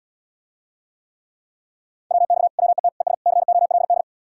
Morse code, heard as a keyed tone, about 77 wpm, 700 Hz. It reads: WX WET ES COLD